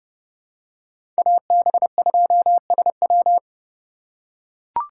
Morse code, heard as a keyed tone, about 32 wpm, 700 Hz. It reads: AB2SW
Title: USA Call Signs - 5 Characters - 4x Character Spacing - 30wpm